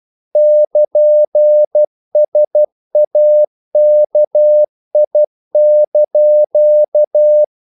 Morse code, heard as a keyed tone, about 12 wpm, 600 Hz. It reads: <KN>SAKI)
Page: English quotations in CW (telegraphy - morse)